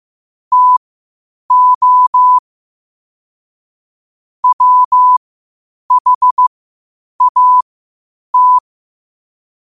Morse code, heard as a keyed tone, about 15 wpm, spaced as for 8 wpm, 1000 Hz. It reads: TO WHAT
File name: qotdfiles / 8wpm